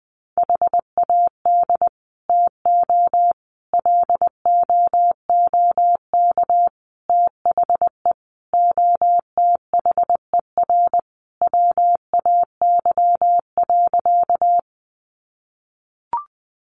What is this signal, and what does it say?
Morse code, heard as a keyed tone, about 20 wpm, 700 Hz. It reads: HAD TO LOOK THE OTHER WAY.
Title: Sentences from Top 100 Words 20wpm